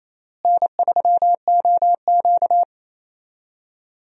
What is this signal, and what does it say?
Morse code, heard as a keyed tone, about 28 wpm, 700 Hz. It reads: N3OQ
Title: USA Call Signs - 4 Characters - Repeated 2x - 28wpm